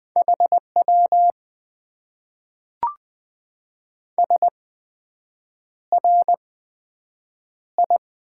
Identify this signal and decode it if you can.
Morse code, heard as a keyed tone, about 20 wpm, spaced as for 7 wpm, 700 Hz. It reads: HW SRI
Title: Top 90 QSO Elements - 8x Character Spacing - 20wpm